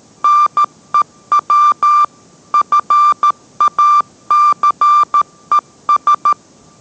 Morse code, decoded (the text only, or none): NEW FACES